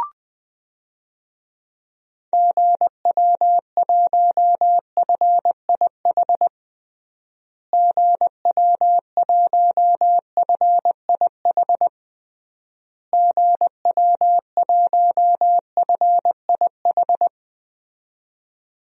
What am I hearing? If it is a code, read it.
Morse code, heard as a keyed tone, about 20 wpm, 700 Hz. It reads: GW1FIH GW1FIH GW1FIH
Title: DXCC-100 Call Signs - 6 Characters - Repeated 3x - 20wpm